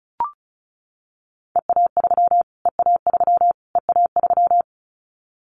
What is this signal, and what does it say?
Morse code, heard as a keyed tone, about 34 wpm, 700 Hz. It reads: EA3 EA3 EA3
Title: Callsign Prefixes for Top 25 DXCC - Repeated 3x - 35wpm